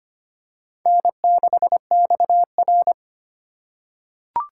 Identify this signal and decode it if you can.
Morse code, heard as a keyed tone, about 25 wpm, 700 Hz. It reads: N6XR
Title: USA Call Signs - 4 Characters - 2x Character Spacing - 25wpm